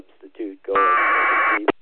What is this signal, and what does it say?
Packet radio (AX.25, as in APRS), with speech.